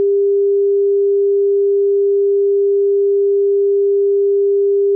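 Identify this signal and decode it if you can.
Dial tone (Japan), 400 Hz.
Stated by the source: Japanese dial tone.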